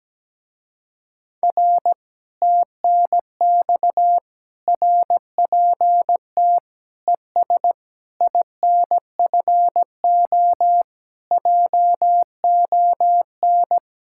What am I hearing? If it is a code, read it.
Morse code, heard as a keyed tone, about 17 wpm, 700 Hz. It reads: R TNX RPT ES INFO JON
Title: Speed-Racing Common QSO Phrases Encore - 17wpm